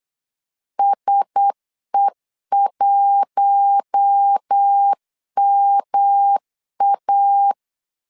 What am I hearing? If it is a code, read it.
Morse code, heard as a keyed tone, about 8 wpm, 800 Hz. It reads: SE1MA
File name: LEZIONI / 6 PKX0 / 6 PK 8wpm 3003